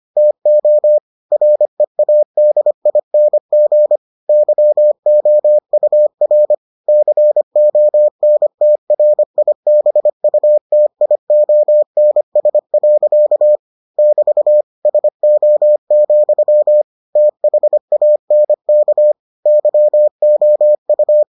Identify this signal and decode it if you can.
Morse code, heard as a keyed tone, about 25 wpm, 600 Hz. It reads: TO READING YOUR CONTRIBUTIONS. <BT> SO, THANK YOU